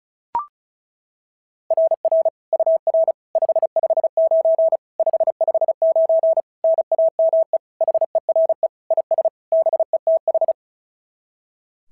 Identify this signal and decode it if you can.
Morse code, heard as a keyed tone, about 36 wpm, 690 Hz. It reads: RR UR 559 559 NAME HERE IS BETH
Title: SOTA QSO Practice 35wpm